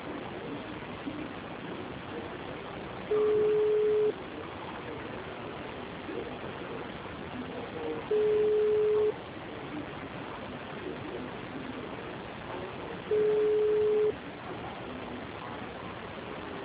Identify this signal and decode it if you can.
Ringback tone (Europe, 425 Hz), with noise.